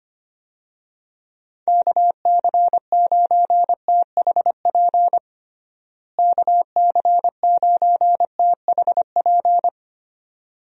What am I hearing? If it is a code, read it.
Morse code, heard as a keyed tone, about 25 wpm, 700 Hz. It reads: KC9THP KC9THP
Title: USA Call Signs - 6 Characters - Repeated 2x - 25wpm